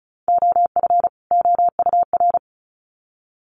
Morse code, heard as a keyed tone, about 34 wpm, 700 Hz. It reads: OF OUR
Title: Sets of 2 Words 35wpm